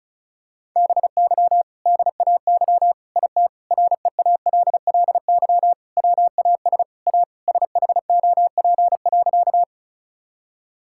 Morse code, heard as a keyed tone, about 34 wpm, 700 Hz. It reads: BY DAY IT REALLY WAS A SHOP.
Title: Sentences from Top 1600 Words 35wpm